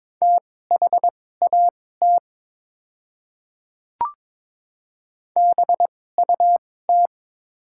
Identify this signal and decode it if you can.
Morse code, heard as a keyed tone, about 22 wpm, spaced as for 16 wpm, 700 Hz. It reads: THAT BUT
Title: 2x Character Spacing - Top 200 Words - 22wpm